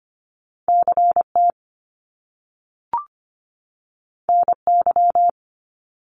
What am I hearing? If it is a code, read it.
Morse code, heard as a keyed tone, about 25 wpm, 700 Hz. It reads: CT NY
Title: Top Two Letter Combinations in Words 25wpm